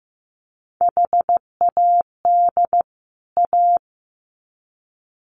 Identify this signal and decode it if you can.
Morse code, heard as a keyed tone, about 15 wpm, 700 Hz. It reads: HAD A
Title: Sets of 2 Words 15wpm